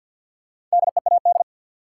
Morse code, heard as a keyed tone, about 52 wpm, 700 Hz. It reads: DEAD
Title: Sentences from Top 1500 Words - New Word Drill - 50wpm